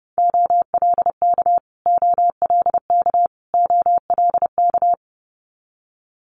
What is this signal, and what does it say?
Morse code, heard as a keyed tone, about 30 wpm, 700 Hz. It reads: OLK OLK OLK
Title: Multi-letter Phonemes Repeated 3x 30wpm